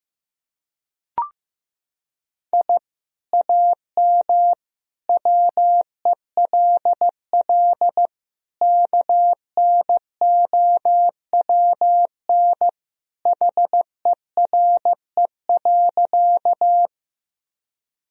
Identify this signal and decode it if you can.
Morse code, heard as a keyed tone, about 16 wpm, 700 Hz. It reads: I AM WELL KNOWN HERE.
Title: Sentences from Top 1000 Words - Review Part 3 - 15wpm